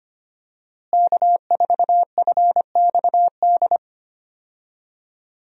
Morse code, heard as a keyed tone, about 25 wpm, 700 Hz. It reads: K4FXD
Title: USA Call Signs - 5 Characters - Repeated 3x - 25wpm